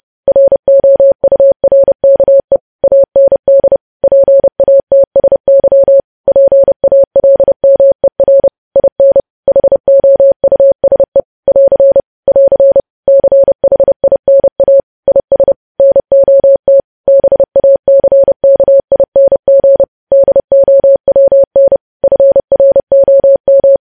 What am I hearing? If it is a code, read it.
Morse code, heard as a keyed tone, about 30 wpm, 550 Hz. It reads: ROURKE AND PATSY PALMER IN HOUSE <AR> <AR> CHINA IS NOT BACKING DOWN FROM
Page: news headlines in morse code at 30 wpm